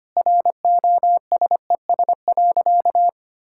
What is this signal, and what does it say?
Morse code, heard as a keyed tone, about 25 wpm, 700 Hz. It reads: ROSES.
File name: Easy Sentences with Hints - 25wpm